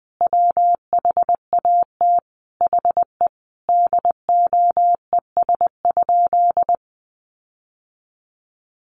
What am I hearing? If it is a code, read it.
Morse code, heard as a keyed tone, about 20 wpm, 700 Hz. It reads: WHAT HE DOES?
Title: Sentences from Top 1300 Words 20wpm